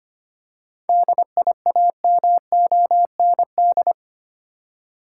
Morse code, heard as a keyed tone, about 25 wpm, 700 Hz. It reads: DIAMOND
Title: Seven Letter Words - 25wpm